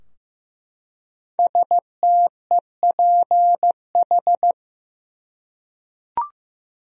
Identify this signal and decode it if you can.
Morse code, heard as a keyed tone, about 15 wpm, 700 Hz. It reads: STEPH
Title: USA Common Names - 15wpm